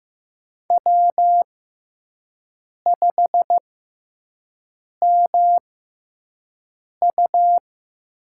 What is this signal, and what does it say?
Morse code, heard as a keyed tone, about 15 wpm, spaced as for 5 wpm, 700 Hz. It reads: W5MU
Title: USA Call Signs - 6x Character Spacing - 15wpm